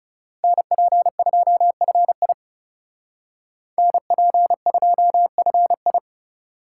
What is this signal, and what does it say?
Morse code, heard as a keyed tone, about 36 wpm, 700 Hz. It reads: NP2FI NP2FI
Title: Speed Racing - USA 2x2 Call Signs 30wpm